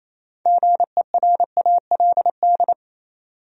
Morse code, heard as a keyed tone, about 28 wpm, 700 Hz. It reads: GERALD
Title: USA Common Names - 28wpm